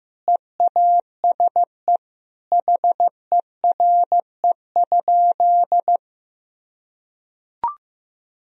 Morse code, heard as a keyed tone, about 16 wpm, 700 Hz. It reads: EASE HERE?
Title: Sentences from Top 1000 Words - Review Part 1 - 15wpm